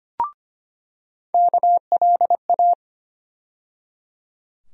Morse code, heard as a keyed tone, about 25 wpm, 700 Hz. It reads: KLA
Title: USA Call Sign Postfixes - 3 Letters 25wpm